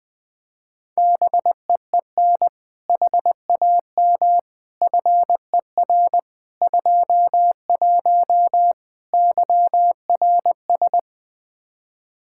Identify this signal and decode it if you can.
Morse code, heard as a keyed tone, about 20 wpm, 700 Hz. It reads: BEEN HAM FER 21 YRS
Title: Practice QSO - Exchange 3 - More Info - 20wpm